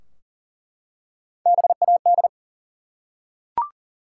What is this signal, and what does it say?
Morse code, heard as a keyed tone, about 38 wpm, 700 Hz. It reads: BAD
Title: Three Letter Words - 40wpm